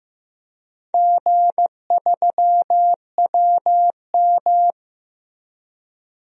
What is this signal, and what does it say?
Morse code, heard as a keyed tone, about 15 wpm, 700 Hz. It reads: G3WM
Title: DXCC-100 Call Signs - 4 Characters - Repeated 2x - 15wpm